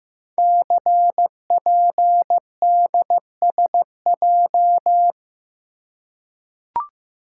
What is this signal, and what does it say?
Morse code, heard as a keyed tone, about 15 wpm, 700 Hz. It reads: CPDSJ
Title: Code Groups - 5 Letter:Numbers - 15wpm